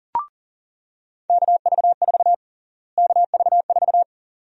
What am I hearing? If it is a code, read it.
Morse code, heard as a keyed tone, about 40 wpm, 700 Hz. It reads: KV4 KV4
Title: USA Call Sign Prefixes - Repeated 2x - 40wpm